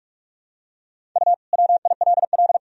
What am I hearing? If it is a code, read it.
Morse code, heard as a keyed tone, about 41 wpm, 700 Hz. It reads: U WILL
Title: Sentences from Top 400 Words and Abbreviations 45wpm